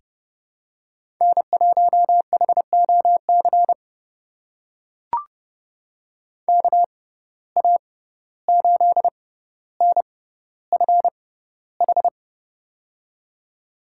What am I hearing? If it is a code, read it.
Morse code, heard as a keyed tone, about 32 wpm, spaced as for 15 wpm, 700 Hz. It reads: N1HOC KA8NFH
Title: USA Call Signs - 6x Character Spacing - 30wpm